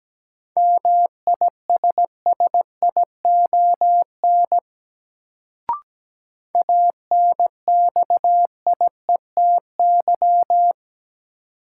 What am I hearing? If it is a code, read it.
Morse code, heard as a keyed tone, about 17 wpm, 700 Hz. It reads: MISSION ANXIETY
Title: Sentences from Top 1600 Words - New Word Drill - 17wpm